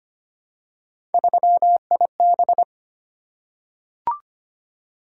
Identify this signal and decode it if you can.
Morse code, heard as a keyed tone, about 25 wpm, 700 Hz. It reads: 3IB
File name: Three Letter-Number Combinations - 25wpm